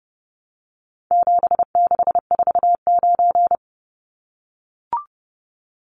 Morse code, heard as a keyed tone, about 29 wpm, 700 Hz. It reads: 7649
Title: Numbers - 4-Digit 30wpm